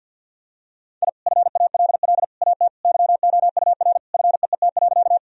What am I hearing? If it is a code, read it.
Morse code, heard as a keyed tone, about 46 wpm, 700 Hz. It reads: I FALL AT YOUR FEET.
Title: Sentences from Top 1000 Words - Review Part 2 - 50wpm